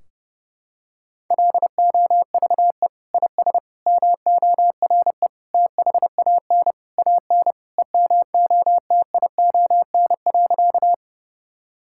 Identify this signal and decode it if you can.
Morse code, heard as a keyed tone, about 30 wpm, 700 Hz. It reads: LOVE IS MORE THAN AN EMOTION.
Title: Sentences from Top 1600 Words 30wpm